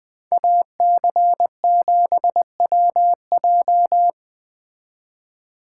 Morse code, heard as a keyed tone, about 20 wpm, 700 Hz. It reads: AC7WJ
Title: Speed Racing - USA 2x2 Call Signs 17wpm